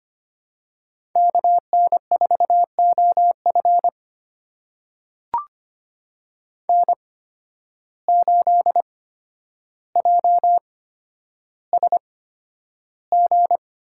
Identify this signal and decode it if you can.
Morse code, heard as a keyed tone, about 25 wpm, spaced as for 11 wpm, 700 Hz. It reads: KN4OF N8JSG
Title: USA Call Signs - 5 Characters - 8x Character Spacing - 25wpm